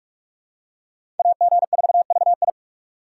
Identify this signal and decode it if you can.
Morse code, heard as a keyed tone, about 46 wpm, 700 Hz. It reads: AG4VI
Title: USA Call Signs - 5 Characters - 6x Character Spacing - 45wpm